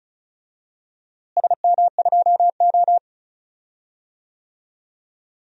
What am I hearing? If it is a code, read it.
Morse code, heard as a keyed tone, about 36 wpm, 700 Hz. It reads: SM2O
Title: DXCC-100 Call Signs - 4 Characters - Repeated 2x - 35wpm